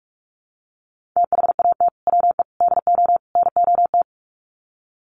Morse code, heard as a keyed tone, about 44 wpm, 700 Hz. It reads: THAT WE DO NOT
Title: Sets of 4 Words 45wpm